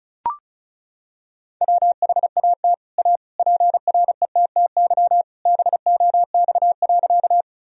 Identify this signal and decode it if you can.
Morse code, heard as a keyed tone, about 34 wpm, 700 Hz. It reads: WHAT A PRETTY BOX.
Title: Sentences from Top 1000 Words - Review Part 2 - 35wpm